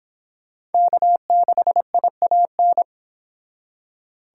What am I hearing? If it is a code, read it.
Morse code, heard as a keyed tone, about 28 wpm, 700 Hz. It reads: K6IAN